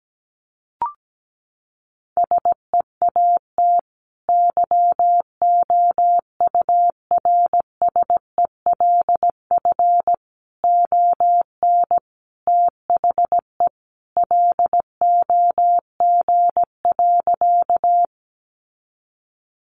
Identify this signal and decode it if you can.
Morse code, heard as a keyed tone, about 17 wpm, 700 Hz. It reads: SEAT YOURSELF ON THE LOG.
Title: Sentences from Top 1500 Words 17wpm